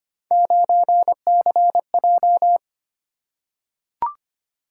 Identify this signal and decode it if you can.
Morse code, heard as a keyed tone, about 25 wpm, 700 Hz. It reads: 9CJ
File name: Three Letter-Number Combinations - 25wpm